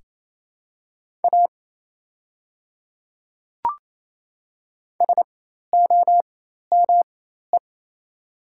Morse code, heard as a keyed tone, about 27 wpm, spaced as for 13 wpm, 700 Hz. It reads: A SOME
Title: 4x Character Spacing - Top 100 Words - 28wpm